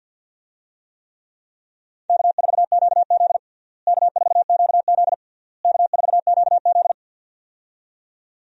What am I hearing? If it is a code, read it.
Morse code, heard as a keyed tone, about 47 wpm, 700 Hz. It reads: K4QZ K4QZ K4QZ
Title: USA Call Signs - 4 Characters - Repeated 3x - 50wpm